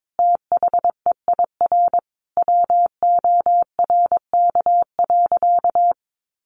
Morse code, heard as a keyed tone, about 22 wpm, 700 Hz. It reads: THEIR WORK.
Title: Sentences from Top 1100 Words 22wpm